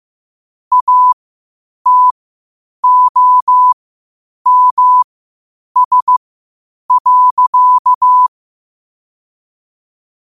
Morse code, heard as a keyed tone, about 15 wpm, spaced as for 8 wpm, 1000 Hz. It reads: ATOMS.
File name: qotdfiles / 8wpm